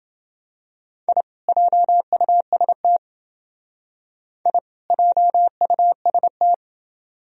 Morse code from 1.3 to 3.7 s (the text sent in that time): JUST